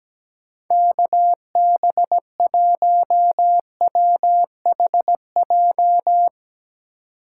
Morse code, heard as a keyed tone, about 19 wpm, 700 Hz. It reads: KB1WHJ
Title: Speed Racing - USA 2x3 Call Signs 17wpm